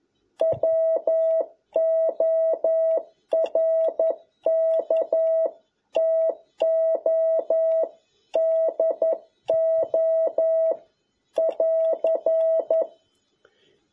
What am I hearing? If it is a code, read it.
Morse code, heard as a keyed tone, about 11 wpm, 650 Hz. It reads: WORK TO DO <AR>
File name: jh8cbh / 2021cw